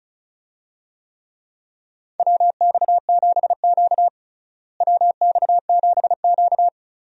Morse code, heard as a keyed tone, about 34 wpm, 700 Hz. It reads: WX7Q WX7Q
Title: USA Call Signs - 4 Characters - Repeated 3x - 35wpm